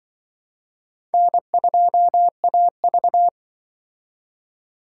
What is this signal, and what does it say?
Morse code, heard as a keyed tone, about 24 wpm, 700 Hz. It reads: N2AV
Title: Speed Racing - USA 1x2 Call Signs 20wpm